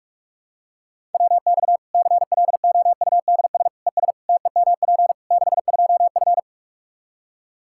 Morse code, heard as a keyed tone, about 41 wpm, 700 Hz. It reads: WX CLOUDS ES TEMP 62F
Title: K1USN SST QSO Practice 45wpm